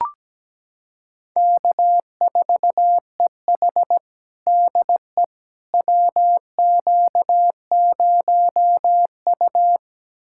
Morse code, heard as a keyed tone, about 17 wpm, 700 Hz. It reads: K4EH DE WQ0U
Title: USA Callsign DE Callsign - 4 Characters - 17wpm